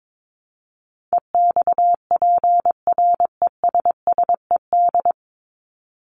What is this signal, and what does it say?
Morse code, heard as a keyed tone, about 22 wpm, 700 Hz. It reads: EXPRESSED
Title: ED Suffix - Top 100 Words with Base Words 22wpm